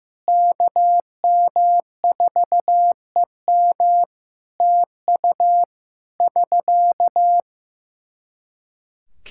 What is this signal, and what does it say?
Morse code, heard as a keyed tone, about 15 wpm, 700 Hz. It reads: KM4EM TU <SK>
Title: Practice QSO - Exchange 4 - Ending - 15wpm